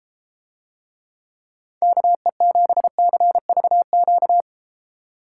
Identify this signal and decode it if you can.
Morse code, heard as a keyed tone, about 31 wpm, 700 Hz. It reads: KE7CVQ